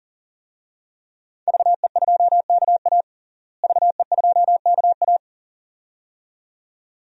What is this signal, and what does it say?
Morse code, heard as a keyed tone, about 37 wpm, 700 Hz. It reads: VE2KA VE2KA
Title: DXCC-100 Call Signs - 5 Characters - Repeated 2x - 40wpm